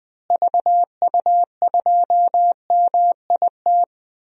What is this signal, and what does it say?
Morse code, heard as a keyed tone, about 20 wpm, 700 Hz. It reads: VU2MIT
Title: DXCC-100 Call Signs - 6 Characters - Repeated 3x - 20wpm